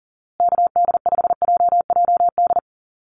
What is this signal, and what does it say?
Morse code, heard as a keyed tone, about 40 wpm, 700 Hz. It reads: KD5JJD
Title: USA Call Signs Repeated 3x 40wpm